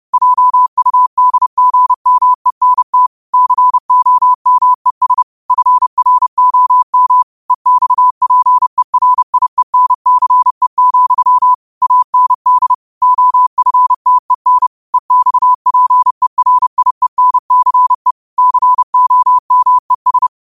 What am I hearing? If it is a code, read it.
Morse code, heard as a keyed tone, about 30 wpm, 1000 Hz. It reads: JUDGMENT COMES FROM EXPERIENCE, AND OFTEN EXPERIENCE COMES
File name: qotdfiles / 30wpm